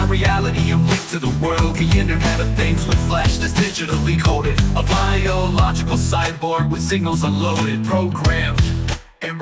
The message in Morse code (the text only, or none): none